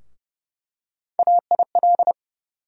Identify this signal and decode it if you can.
Morse code, heard as a keyed tone, about 29 wpm, 700 Hz. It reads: AIL
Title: USA Call Sign Postfixes - 3 Letters 30wpm